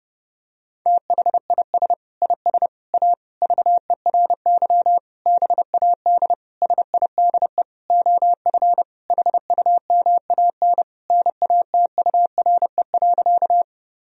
Morse code, heard as a keyed tone, about 30 wpm, 700 Hz. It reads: THIS IS A VERY BAD SIDE OF HUMAN NATURE.
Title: Sentences from Top 1000 Words - Review Part 3 - 30wpm